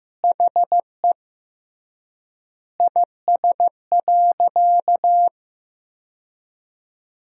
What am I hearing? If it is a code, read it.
Morse code, heard as a keyed tone, about 16 wpm, 700 Hz. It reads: HE IS.
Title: 3x Word-Spacing - Sentences from Top 200 Words - 15wpm